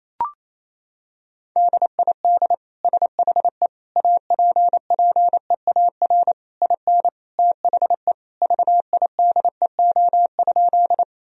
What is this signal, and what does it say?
Morse code, heard as a keyed tone, about 28 wpm, 700 Hz. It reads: DID SHE APPEAR IN THE VIDEO?
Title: Sentences from Top 1000 Words - Review Part 4 - 28wpm